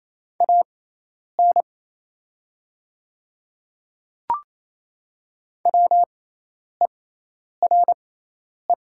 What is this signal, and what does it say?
Morse code, heard as a keyed tone, about 27 wpm, spaced as for 10 wpm, 690 Hz. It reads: AN WERE
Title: 6x Character Spacing - Top 100 Words - 28wpm